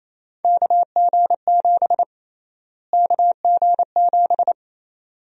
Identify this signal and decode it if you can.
Morse code, heard as a keyed tone, about 27 wpm, 700 Hz. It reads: KG7 KG7
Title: USA Call Sign Prefixes - Repeated 2x - 28wpm